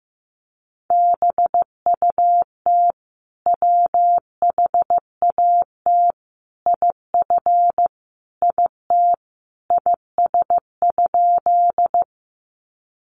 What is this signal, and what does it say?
Morse code, heard as a keyed tone, about 15 wpm, 700 Hz. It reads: BUT WHAT IF IT IS?
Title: Sentences from Top 100 Words 15wpm